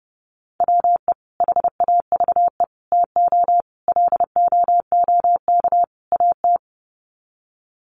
Morse code, heard as a keyed tone, about 30 wpm, 700 Hz. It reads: WE HAVE TO LOOK AT
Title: Sets of 5 Words Encore2 30wpm